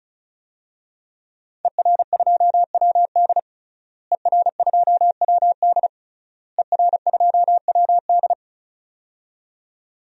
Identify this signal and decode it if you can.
Morse code, heard as a keyed tone, about 36 wpm, 700 Hz. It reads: ER2WD ER2WD ER2WD
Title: DXCC-100 Call Signs - 5 Characters - Repeated 3x - 35wpm